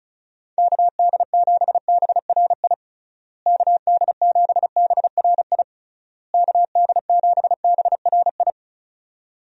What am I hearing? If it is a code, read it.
Morse code, heard as a keyed tone, about 36 wpm, 700 Hz. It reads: KD7BRI KD7BRI KD7BRI
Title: USA Call Signs - 6 Characters - Repeated 3x - 35wpm